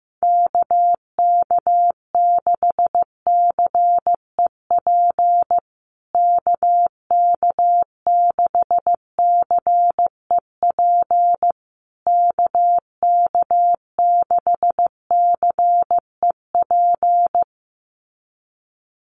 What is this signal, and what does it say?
Morse code, heard as a keyed tone, about 15 wpm, 700 Hz. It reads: KK6CEP KK6CEP KK6CEP
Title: USA Call Signs Repeated 3x 15wpm